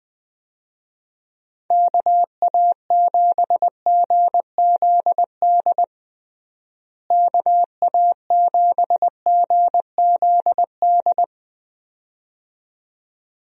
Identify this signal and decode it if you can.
Morse code, heard as a keyed tone, about 20 wpm, 700 Hz. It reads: KA7GZD KA7GZD
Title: USA Call Signs - 6 Characters - Repeated 2x - 20wpm